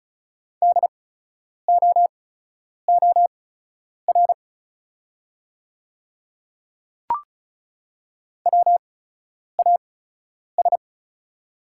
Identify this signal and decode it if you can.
Morse code, heard as a keyed tone, about 36 wpm, spaced as for 10 wpm, 690 Hz. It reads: DOOR WAS
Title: 8x Character Spacing - Top 500 Words - 35wpm